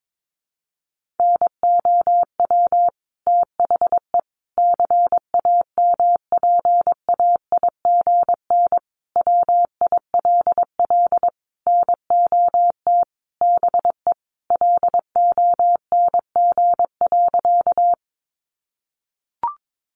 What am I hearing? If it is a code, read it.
Morse code, heard as a keyed tone, about 22 wpm, 700 Hz. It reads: NOW THE CAMPAIGN WILL NOT BE LONG.
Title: Sentences from Top 1200 Words 22wpm